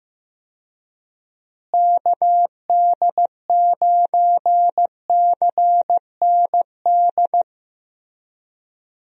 Morse code, heard as a keyed tone, about 15 wpm, 700 Hz. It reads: KD9CND
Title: USA Call Signs - 6 Characters - Repeated 2x - 15wpm